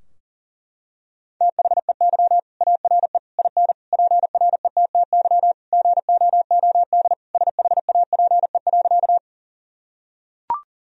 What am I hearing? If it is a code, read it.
Morse code, heard as a keyed tone, about 40 wpm, 700 Hz. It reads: THEY ARE IN PRETTY GOOD SHAPE.
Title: Sentences from Top 1000 Words - Review Part 3 - 40wpm